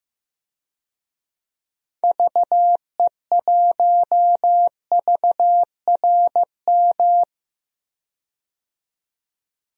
Morse code, heard as a keyed tone, about 15 wpm, 700 Hz. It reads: VE1VRM
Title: DXCC-100 Call Signs - 6 Characters - 15wpm